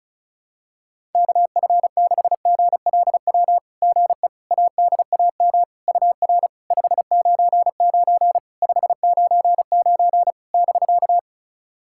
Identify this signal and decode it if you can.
Morse code, heard as a keyed tone, about 34 wpm, 690 Hz. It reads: KF6GLW GE ADAM UR 599 599 <BK>
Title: SOTA QSO Practice 35wpm